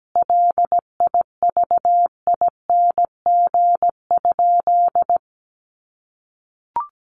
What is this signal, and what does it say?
Morse code, heard as a keyed tone, about 17 wpm, 700 Hz. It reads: LIVING?
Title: Sentences from Top 1100 Words 17wpm